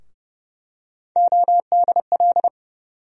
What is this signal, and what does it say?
Morse code, heard as a keyed tone, about 30 wpm, 700 Hz. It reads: ODL
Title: USA Call Sign Postfixes - 3 Letters 30wpm